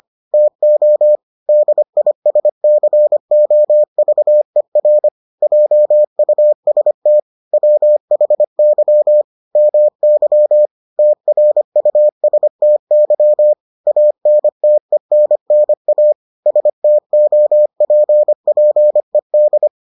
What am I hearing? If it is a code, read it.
Morse code, heard as a keyed tone, about 25 wpm, 600 Hz. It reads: TO DISCOVER JUST WHY MY TRUSTY ANTENNA STOPPED